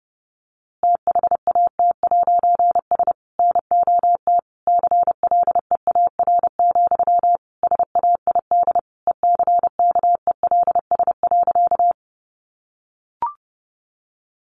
Morse code, heard as a keyed tone, about 30 wpm, 700 Hz. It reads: THAT'S NOT CLEAR, SAID ECKELS.
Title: A Sound Of Thunder 30wpm